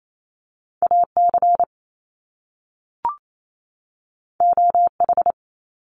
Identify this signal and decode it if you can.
Morse code, heard as a keyed tone, about 28 wpm, 700 Hz. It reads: AC OH
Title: Top Two Letter Combinations in Words 28wpm